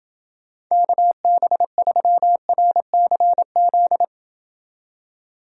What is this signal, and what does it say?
Morse code, heard as a keyed tone, about 29 wpm, 700 Hz. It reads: KB3RCZ